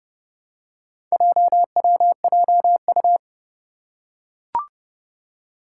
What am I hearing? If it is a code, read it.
Morse code, heard as a keyed tone, about 30 wpm, 700 Hz. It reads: JWJU
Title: Four Challenging Letter-Number Combinations 30wpm